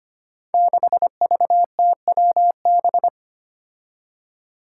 Morse code, heard as a keyed tone, about 25 wpm, 700 Hz. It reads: 6VTWB
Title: Five Challenging Letter-Number Combinations 25wpm